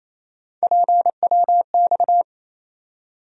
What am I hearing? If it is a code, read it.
Morse code, heard as a keyed tone, about 28 wpm, 700 Hz. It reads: PWX
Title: USA Call Sign Postfixes - 3 Letters 28wpm